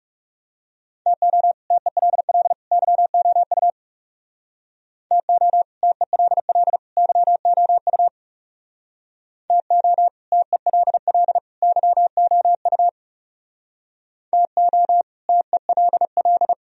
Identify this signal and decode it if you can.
Morse code, heard as a keyed tone, about 36 wpm, 700 Hz. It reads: TO TELL YOU TO TELL YOU TO TELL YOU TO TELL
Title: Speed-Racing - Sets of 3 Words 30wpm